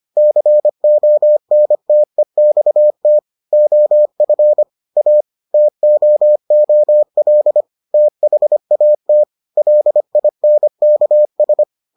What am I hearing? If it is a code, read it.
Morse code, heard as a keyed tone, about 25 wpm, 600 Hz. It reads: CONTEXT OF A TOOL THAT LINKS